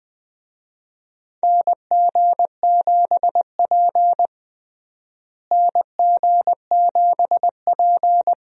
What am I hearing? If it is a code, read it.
Morse code, heard as a keyed tone, about 20 wpm, 700 Hz. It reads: NG7P NG7P
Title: USA Call Signs - 4 Characters - Repeated 2x - 20wpm